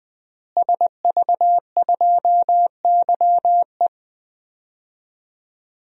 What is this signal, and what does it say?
Morse code, heard as a keyed tone, about 20 wpm, 700 Hz. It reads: SV2YE
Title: DXCC-100 Call Signs - 5 Characters - Repeated 3x - 20wpm